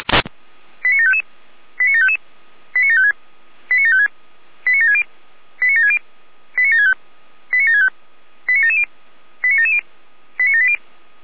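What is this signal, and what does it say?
Five-tone selective call (ZVEI), heard as quick runs of five tones, tones of about 70 ms. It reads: 89755 89755 89756 89756 89760 89760 89765 89765 89770 89770 89780
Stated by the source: Five-tone ZVEI selective call received on an SDR.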